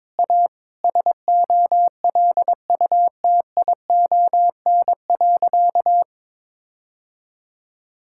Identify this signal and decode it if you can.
Morse code, heard as a keyed tone, about 22 wpm, 700 Hz. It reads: A SOLUTION.
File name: practice / 22wpm / Sentences with 8-Word Phrases - 22wpm